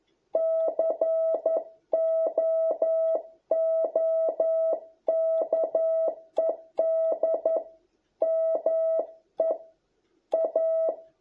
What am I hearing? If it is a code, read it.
Morse code, heard as a keyed tone, about 11 wpm, 660 Hz. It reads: COOKED ME A